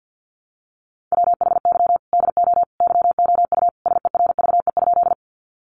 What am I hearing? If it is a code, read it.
Morse code, heard as a keyed tone, about 50 wpm, 700 Hz. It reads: WHY DO YOU SERVE?
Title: Sentences from Top 400 Words 50wpm